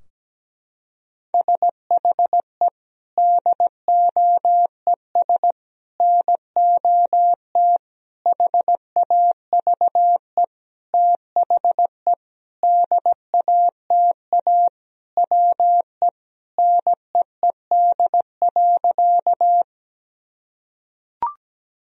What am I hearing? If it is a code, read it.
Morse code, heard as a keyed tone, about 17 wpm, 700 Hz. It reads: SHE DOES NOT HAVE THE DATA WE NEED.
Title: Sentences from Top 1000 Words - Review Part 3 - 17wpm